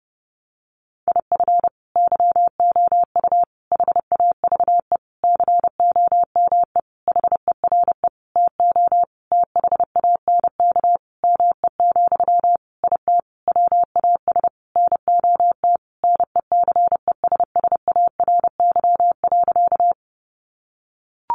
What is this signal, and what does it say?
Morse code, heard as a keyed tone, about 30 wpm, 700 Hz. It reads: IF YOU HAVE COME HERE TO THANK ME, IT WAS NOT NECESSARY.
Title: Sentences from Top 800 Words 30wpm